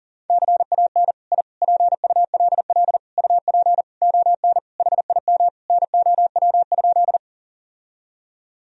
Morse code, heard as a keyed tone, about 40 wpm, 700 Hz. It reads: CAN I PULL UP ON HIM NOW?
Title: Sentences from Top 1000 Words - Review Part 2 - 40wpm